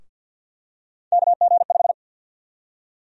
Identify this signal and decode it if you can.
Morse code, heard as a keyed tone, about 47 wpm, 700 Hz. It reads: KG5
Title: USA Call Sign Prefixes - 50wpm